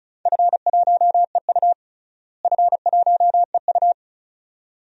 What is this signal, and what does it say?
Morse code, heard as a keyed tone, about 36 wpm, 700 Hz. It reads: F1EU F1EU
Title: DXCC-100 Call Signs - 4 Characters - Repeated 2x - 35wpm